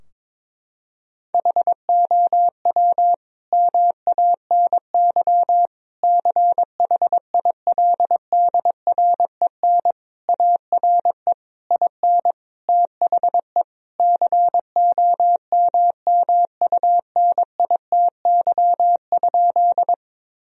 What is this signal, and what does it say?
Morse code, heard as a keyed tone, about 22 wpm, 700 Hz. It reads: HOW MANY CHILDREN ARE IN THE COMMUNITY?
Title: Sentences from Top 1000 Words - Review Part 4 - 22wpm